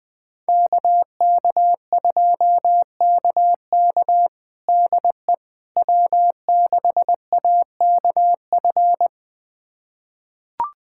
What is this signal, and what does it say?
Morse code, heard as a keyed tone, about 21 wpm, 700 Hz. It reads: KK2KK DE W6AKF
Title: USA Callsign DE Callsign - 5 Characters - 20wpm